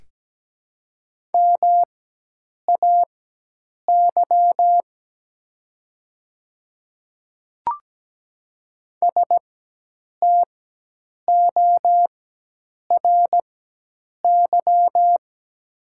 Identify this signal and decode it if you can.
Morse code, heard as a keyed tone, about 17 wpm, spaced as for 8 wpm, 700 Hz. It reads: MAY STORY
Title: 4x Character Spacing - Top 300 Words - 17wpm